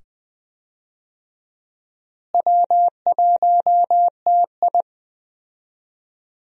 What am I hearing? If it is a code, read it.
Morse code, heard as a keyed tone, about 20 wpm, 700 Hz. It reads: W1TI
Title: USA Call Signs - 4 Characters - Repeated 2x - 20wpm